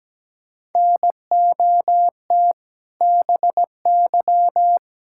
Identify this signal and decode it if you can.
Morse code, heard as a keyed tone, about 17 wpm, 700 Hz. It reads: NOT BY
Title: Sentences from Top 1000 Words - Review Part 5 - 17wpm